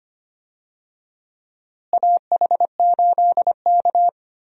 Morse code, heard as a keyed tone, about 25 wpm, 700 Hz. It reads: AH8K
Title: USA Call Signs - 4 Characters - Repeated 3x - 25wpm